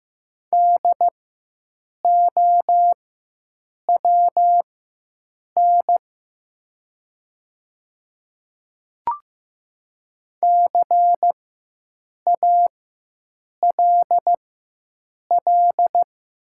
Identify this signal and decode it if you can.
Morse code, heard as a keyed tone, about 15 wpm, spaced as for 7 wpm, 700 Hz. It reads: DOWN CALL
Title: 4x Character Spacing - Top 100 Words - 15wpm